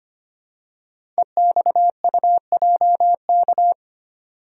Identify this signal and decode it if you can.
Morse code, heard as a keyed tone, about 25 wpm, 700 Hz. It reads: EXUJK
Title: Code Groups - 5 Letters - 25wpm